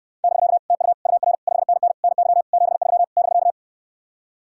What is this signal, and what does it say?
Morse code, heard as a keyed tone, about 69 wpm, 700 Hz. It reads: <BK> TU UR 5NN NY 73 <BK>